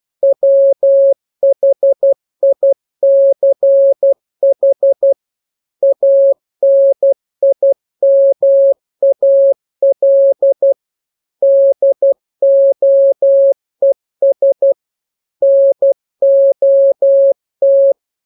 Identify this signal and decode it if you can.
Morse code, heard as a keyed tone, about 12 wpm, 550 Hz. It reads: WHICH ANIMAL DOES NOT